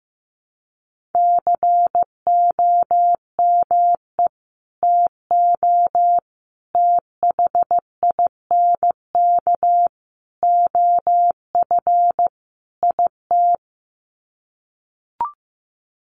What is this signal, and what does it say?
Morse code, heard as a keyed tone, about 15 wpm, 700 Hz. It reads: COME TO THINK OF IT
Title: Sets of 5 Words Encore 15wpm